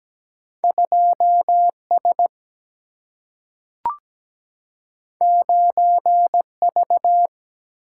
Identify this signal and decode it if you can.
Morse code, heard as a keyed tone, about 17 wpm, 700 Hz. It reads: 2 S 9 V
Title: Single-Letter Single-Number 17wpm